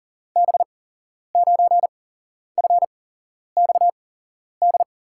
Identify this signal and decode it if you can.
Morse code, heard as a keyed tone, about 39 wpm, spaced as for 11 wpm, 700 Hz. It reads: B9FXD
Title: USA Call Signs - 6 Characters - 8x Character Spacing - 40wpm